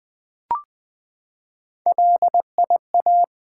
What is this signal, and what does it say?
Morse code, heard as a keyed tone, about 20 wpm, 700 Hz. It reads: LIA
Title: Build A Word - 3 Letters At A Time - 6-Letter Words - 20wpm